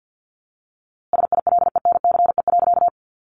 Morse code, heard as a keyed tone, about 50 wpm, 700 Hz. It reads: SILENCE.
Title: A Sound Of Thunder 50wpm